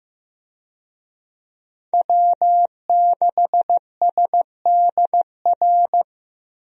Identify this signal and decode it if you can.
Morse code, heard as a keyed tone, about 16 wpm, 700 Hz. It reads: W6SDR
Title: USA Call Signs - 5 Characters - Repeated 2x - 15wpm